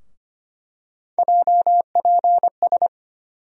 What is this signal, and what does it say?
Morse code, heard as a keyed tone, about 25 wpm, 700 Hz. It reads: JPS
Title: USA Call Sign Postfixes - 3 Letters 25wpm